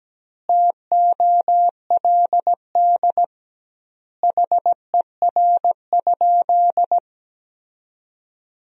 Morse code, heard as a keyed tone, about 17 wpm, 700 Hz. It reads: TOLD HER?
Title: 2x Word-Spacing - Sentences from Top 400 Words - 17wpm